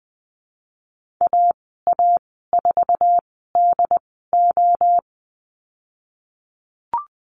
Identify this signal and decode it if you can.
Morse code, heard as a keyed tone, about 20 wpm, spaced as for 15 wpm, 700 Hz. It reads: AA4DO
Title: 076 Lesson 13g - Letter L - Call Signs 20wpm